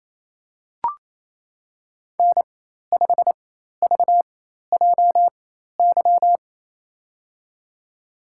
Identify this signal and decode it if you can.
Morse code, heard as a keyed tone, about 28 wpm, spaced as for 13 wpm, 700 Hz. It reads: N5VJY
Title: USA Call Signs - 5 Characters - 4x Character Spacing - 28wpm